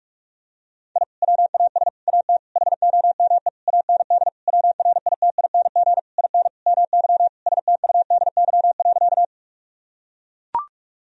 Morse code, heard as a keyed tone, about 41 wpm, 700 Hz. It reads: I WAS AT HOME AND WRITING IN MY STUDY.
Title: Sentences from Top 1000 Words - Review Part 1 - 45wpm